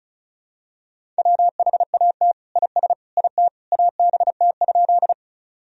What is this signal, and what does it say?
Morse code, heard as a keyed tone, about 36 wpm, 700 Hz. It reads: WHAT IS IT ABT?
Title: Sentences from Top 100 Words and Abbreviations 35wpm